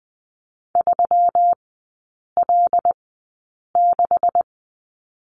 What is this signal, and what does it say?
Morse code, heard as a keyed tone, about 20 wpm, spaced as for 8 wpm, 700 Hz. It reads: 3L6
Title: 159 Lesson 26e - Letter g - Three Characters 20wpm